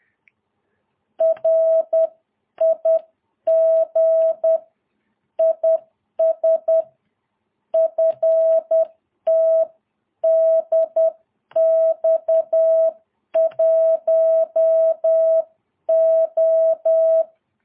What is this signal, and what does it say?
Morse code, heard as a keyed tone, about 10 wpm, 650 Hz. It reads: RIG IS FTDX1O
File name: jh8cbh / 2022cw